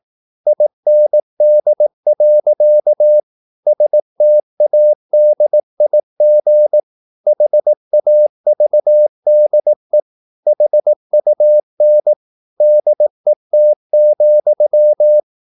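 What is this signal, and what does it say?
Morse code, heard as a keyed tone, about 18 wpm, 600 Hz. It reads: IND. STADIG HAVDE HUN DET,
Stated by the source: Slow (QRS) Morse practice at 18 wpm, CW med Gnister.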